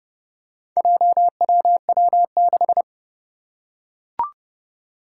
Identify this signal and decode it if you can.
Morse code, heard as a keyed tone, about 30 wpm, 700 Hz. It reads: JWW6
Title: Four Challenging Letter-Number Combinations 30wpm